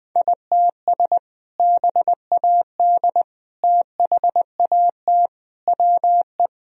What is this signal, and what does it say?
Morse code, heard as a keyed tone, about 21 wpm, 700 Hz. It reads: ITS BAD THAT WE
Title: Sentences from Top 1000 Words - Review Part 1 - 20wpm